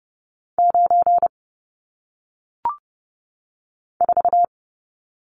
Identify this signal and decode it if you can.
Morse code, heard as a keyed tone, about 30 wpm, 700 Hz. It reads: 9 4
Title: Numbers - 1-Digit 30wpm